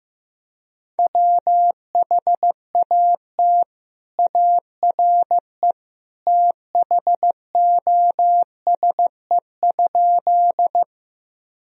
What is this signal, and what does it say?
Morse code, heard as a keyed tone, about 15 wpm, 700 Hz. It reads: WHAT ARE THOSE?
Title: Sentences from Top 1000 Words - Review Part 1 - 15wpm